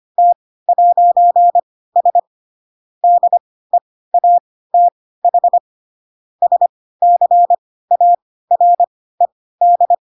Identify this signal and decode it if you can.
Morse code, heard as a keyed tone, about 25 wpm, spaced as for 16 wpm, 700 Hz. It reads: T'S DEATH SCARED